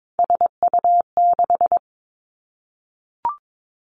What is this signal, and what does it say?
Morse code, heard as a keyed tone, about 22 wpm, 700 Hz. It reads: SU6
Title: Three Challenging Letter-Number Combinations 22wpm